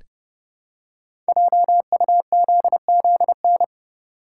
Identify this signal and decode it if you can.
Morse code, heard as a keyed tone, about 30 wpm, 700 Hz. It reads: JUZZN
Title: Code Groups - 5 Letters - 30wpm